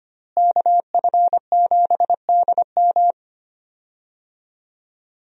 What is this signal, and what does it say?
Morse code, heard as a keyed tone, about 25 wpm, 700 Hz. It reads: KF7DM
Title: USA Call Signs - 5 Characters - Repeated 3x - 25wpm